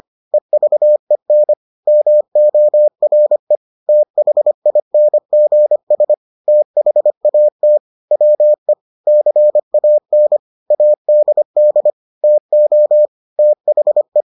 Morse code, heard as a keyed tone, about 25 wpm, 600 Hz. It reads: EVEN MORE THINGS THAT WE CAN ADD TO THE